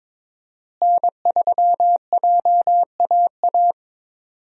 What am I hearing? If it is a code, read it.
Morse code, heard as a keyed tone, about 22 wpm, 700 Hz. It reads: N3JAA
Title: USA Call Signs - 5 Characters - Repeated 2x - 22wpm